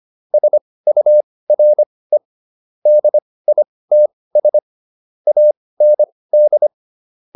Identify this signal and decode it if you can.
Morse code, heard as a keyed tone, about 25 wpm, spaced as for 18 wpm, 600 Hz. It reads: SURE DITS AND